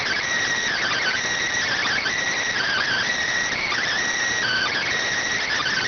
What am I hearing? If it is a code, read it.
Emergency Alert System (SAME) tones, heard as harsh buzzing bursts. It is heard from 0.7 to 4.9 s.
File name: findings / CT AXIAL -572mm 2.gen-38 rx